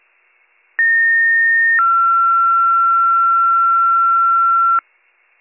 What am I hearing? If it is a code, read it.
Two-tone page, heard as a one-second tone, then a three-second tone, about 1850 then 1360 Hz. It is heard from 0.8 to 4.8 s.